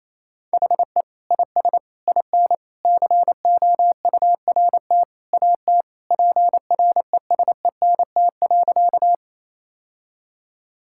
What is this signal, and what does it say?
Morse code, heard as a keyed tone, about 28 wpm, 700 Hz. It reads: HE IS IN COURT AT PRESENT.
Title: Sentences from Top 1600 Words 28wpm